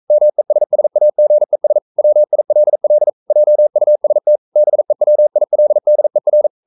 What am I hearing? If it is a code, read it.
Morse code, heard as a keyed tone, about 42 wpm, 600 Hz. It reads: MESSAGES WILL JUST BEWILDER
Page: Romanen Never af Ken Follet på engelsk.